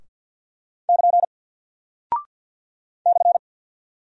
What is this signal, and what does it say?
Morse code, heard as a keyed tone, about 48 wpm, 700 Hz. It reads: <KN> /
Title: Single Character (Letter:Number:Punctuation:Prosign) Alt 50wpm